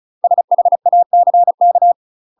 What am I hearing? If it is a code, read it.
Morse code, heard as a keyed tone, about 36 wpm, 700 Hz. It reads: SHACK
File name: UK MASTER QSOs TRAINING / 35WPM / UK MASTER QSOs 0011